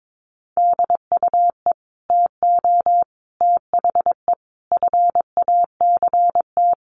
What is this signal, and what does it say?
Morse code, heard as a keyed tone, about 22 wpm, 700 Hz. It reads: DUE TO THE FACT
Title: Sets of 5 Words Encore 22wpm